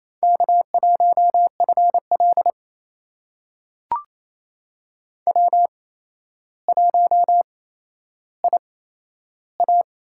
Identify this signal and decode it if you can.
Morse code, heard as a keyed tone, about 28 wpm, spaced as for 12 wpm, 700 Hz. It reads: K1FL W1IA
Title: USA Call Signs - 4 Characters - 8x Character Spacing - 28wpm